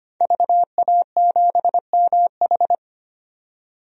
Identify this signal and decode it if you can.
Morse code, heard as a keyed tone, about 25 wpm, 700 Hz. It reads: VA7MH
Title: DXCC-100 Call Signs - 5 Characters - 25wpm